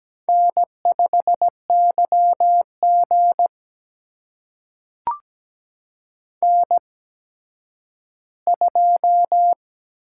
Morse code, heard as a keyed tone, about 17 wpm, spaced as for 10 wpm, 700 Hz. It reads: N5YG N2
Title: USA Call Signs - 4 Characters - 8x Character Spacing - 17wpm